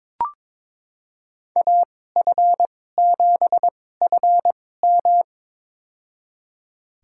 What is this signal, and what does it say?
Morse code, heard as a keyed tone, about 22 wpm, spaced as for 16 wpm, 700 Hz. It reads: AF7FM
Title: USA Call Signs - 5 Characters - 2x Character Spacing - 22wpm